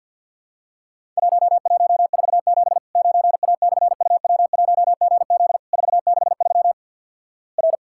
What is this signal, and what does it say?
Morse code, heard as a keyed tone, about 50 wpm, 700 Hz. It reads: 1147 9A/UW1GZ 463 R
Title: Worked All Europe DX Contest - QTC Exchange with Repeats - 50wpm